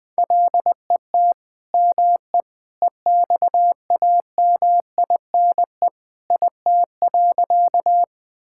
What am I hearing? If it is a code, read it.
Morse code, heard as a keyed tone, about 21 wpm, 700 Hz. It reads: LET ME EXAMINE IT.
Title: Sentences from Top 1600 Words 20wpm